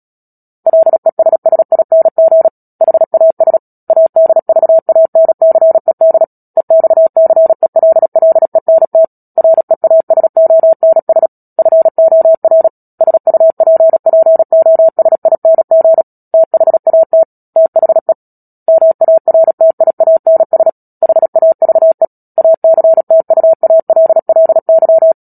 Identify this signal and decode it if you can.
Morse code, heard as a keyed tone, about 37 wpm, 650 Hz. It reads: LESSING HAS ADVANCED EXCELLENT REASONS FOR SUPPOSING THAT THE MARTIANS HAVE ACTUALLY
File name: Book Two / 10-Chapter 10 - Epilogue - 36 WPM